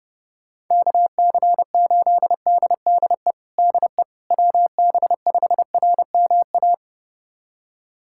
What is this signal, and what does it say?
Morse code, heard as a keyed tone, about 32 wpm, 700 Hz. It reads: KC8DDE DE WB5RMA
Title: USA Callsign DE Callsign - 6 Characters - 30wpm